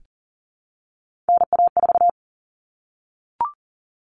Morse code, heard as a keyed tone, about 40 wpm, 700 Hz. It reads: NA4
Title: All Cut Numbers - 3-Digit 40wpm